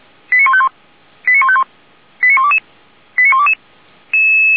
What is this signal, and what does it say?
Five-tone selective call (ZVEI), heard as quick runs of five tones, tones of about 70 ms. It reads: 89151 89151 89122 89122
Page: /464/SDR/ZVEI/85371/2022.08.16/